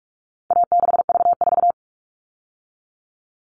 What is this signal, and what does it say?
Morse code, heard as a keyed tone, about 44 wpm, 700 Hz. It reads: A6V4
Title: All Cut Numbers - 4-Digit 45wpm